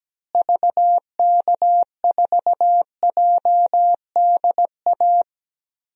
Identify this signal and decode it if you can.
Morse code, heard as a keyed tone, about 17 wpm, 700 Hz. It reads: VK4JDA
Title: DXCC-100 Call Signs - 6 Characters - Repeated 3x - 17wpm